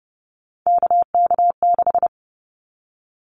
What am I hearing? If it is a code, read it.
Morse code, heard as a keyed tone, about 29 wpm, 700 Hz. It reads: KK6
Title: Callsign Prefixes for Top 25 DXCC - Repeated 3x - 30wpm